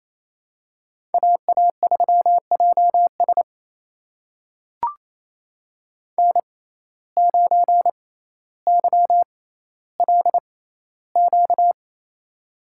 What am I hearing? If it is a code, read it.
Morse code, heard as a keyed tone, about 28 wpm, spaced as for 14 wpm, 700 Hz. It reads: AA3JS N9YLQ
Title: USA Call Signs - 5 Characters - 6x Character Spacing - 28wpm